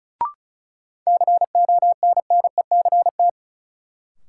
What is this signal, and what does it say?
Morse code, heard as a keyed tone, about 36 wpm, 700 Hz. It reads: CONNECT
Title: Sentences from Top 1500 Words - New Word Drill Alt - 35wpm